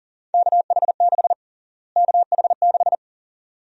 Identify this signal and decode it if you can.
Morse code, heard as a keyed tone, about 40 wpm, 700 Hz. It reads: KH6 KH6
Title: USA Call Sign Prefixes - Repeated 2x - 40wpm